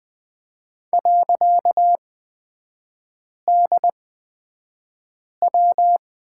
Morse code, heard as a keyed tone, about 20 wpm, spaced as for 5 wpm, 700 Hz. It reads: .DW
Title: 255 Lesson 41e - Period - Three Characters 20wpm